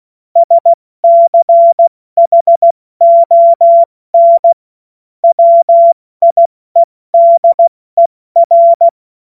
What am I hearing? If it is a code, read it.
Morse code, heard as a keyed tone, about 16 wpm, 680 Hz. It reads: SCHON WIEDER
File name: Morsekurs / Witze / Witz 24. Januar 2023 70 F80